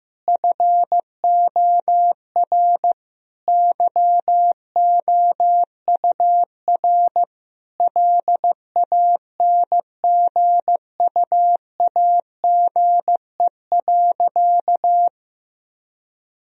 Morse code, heard as a keyed tone, about 14 wpm, 700 Hz. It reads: FOR YOUR LANGUAGE.
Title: Sentences from Top 1000 Words - Review Part 3 - 15wpm